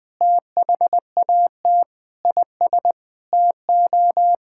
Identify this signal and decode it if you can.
Morse code, heard as a keyed tone, about 20 wpm, 700 Hz. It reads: THAT IS TO
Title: Sentences with 8-Word Phrases - 20wpm